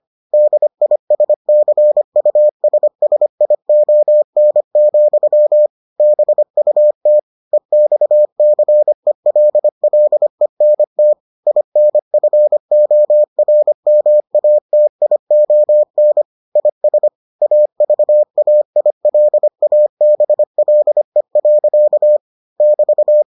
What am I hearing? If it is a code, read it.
Morse code, heard as a keyed tone, about 25 wpm, 600 Hz. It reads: DISCUSSION, BUT EXCELLENT INFORMATION IS AVAILABLE. <BT>